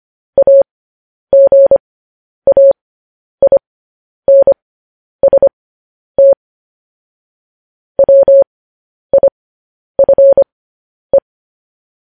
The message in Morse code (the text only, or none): AGAINST WIFE